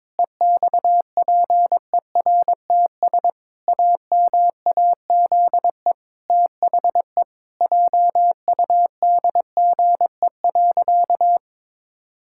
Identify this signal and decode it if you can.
Morse code, heard as a keyed tone, about 22 wpm, 700 Hz. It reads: EXPERTS AMAZE THE JUDGE.
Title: Pangram Challenge v2 - 22wpm